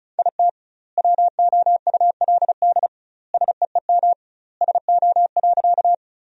Morse code, heard as a keyed tone, about 36 wpm, 700 Hz. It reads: IT WOULD SEEM SO.
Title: 2x Word-Spacing - Sentences from Top 300 Words - 35wpm